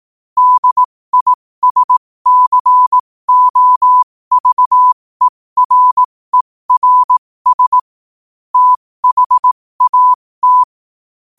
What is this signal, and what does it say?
Morse code, heard as a keyed tone, about 18 wpm, spaced as for 15 wpm, 1000 Hz. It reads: DISCOVERERS THAT
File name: qotdfiles / 15wpm